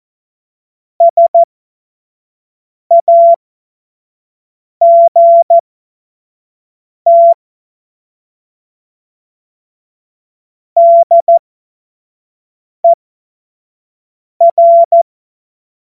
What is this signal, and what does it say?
Morse code, heard as a keyed tone, about 14 wpm, spaced as for 5 wpm, 680 Hz. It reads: SAGT DER
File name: Morsekurs / Witze / Witz 12 Dezember 2023 25 F70